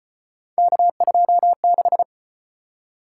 Morse code, heard as a keyed tone, about 34 wpm, 700 Hz. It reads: K26